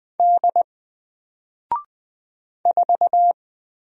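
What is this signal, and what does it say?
Morse code, heard as a keyed tone, about 20 wpm, 700 Hz. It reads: D 4
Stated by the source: Single letter and number Morse drill at 20 wpm, from N4LCD.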